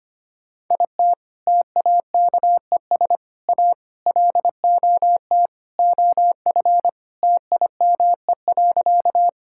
Morse code, heard as a keyed tone, about 25 wpm, 700 Hz. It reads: IT TAKES A LOT OF TIME.
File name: practice / 25wpm / Sentences with 4-Word Phrases - 25wpm